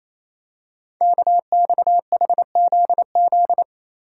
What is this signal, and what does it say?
Morse code, heard as a keyed tone, about 27 wpm, 700 Hz. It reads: KXHZZ
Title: Code Groups - 5 Letters - 28wpm